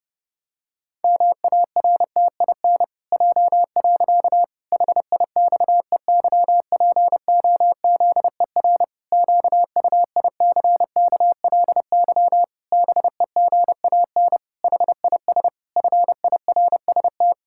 Morse code, heard as a keyed tone, about 30 wpm, 700 Hz. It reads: MARTIN J. HIXEYPOZER QUICKLY BEGAN HIS FIRST
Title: Pangram Challenge v2 - 30wpm